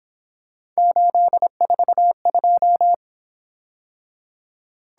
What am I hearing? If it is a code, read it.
Morse code, heard as a keyed tone, about 26 wpm, 700 Hz. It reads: 842
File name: Speed-Racing - Three Letter-Number Combinations - 22wpm